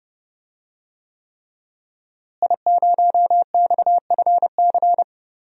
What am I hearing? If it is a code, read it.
Morse code, heard as a keyed tone, about 30 wpm, 700 Hz. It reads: I0XFC
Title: DXCC-100 Call Signs - 5 Characters - Repeated 3x - 30wpm